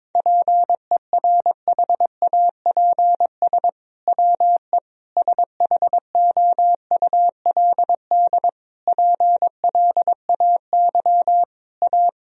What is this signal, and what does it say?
Morse code, heard as a keyed tone, about 22 wpm, 700 Hz. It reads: PERHAPS WE SHOULD PLAY A
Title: Sentences from Top 1000 Words - Review Part 3 - 22wpm